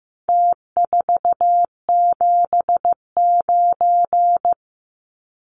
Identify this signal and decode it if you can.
Morse code, heard as a keyed tone, about 15 wpm, 700 Hz. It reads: T479
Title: Numbers - Contest Style - 4-Digit 15wpm